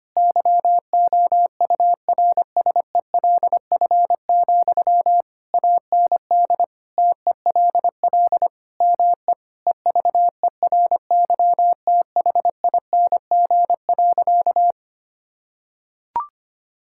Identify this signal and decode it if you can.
Morse code, heard as a keyed tone, about 25 wpm, 700 Hz. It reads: YOURSELF, AND TELL ME EVERYTHING.
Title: Sentences from Top 1600 Words 25wpm